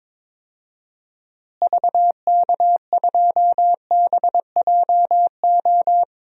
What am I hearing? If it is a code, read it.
Morse code, heard as a keyed tone, about 22 wpm, 700 Hz. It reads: VK2BJO
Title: DXCC-100 Call Signs - 6 Characters - Repeated 2x - 22wpm